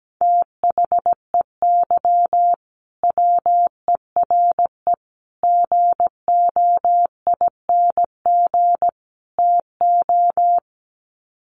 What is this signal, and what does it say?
Morse code, heard as a keyed tone, about 17 wpm, 700 Hz. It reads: THEY WERE GOING TO
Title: Sets of 4 Words 17wpm